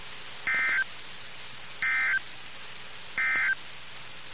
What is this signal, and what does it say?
Emergency Alert System (SAME) tones, heard as harsh buzzing bursts. It is heard from 0.4 to 3.6 s.
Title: NOAA Weather Radio - KIH61